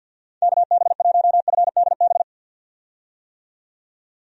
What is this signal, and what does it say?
Morse code, heard as a keyed tone, about 53 wpm, 700 Hz. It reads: KB1FDB
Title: USA Call Signs - 6 Characters - 50wpm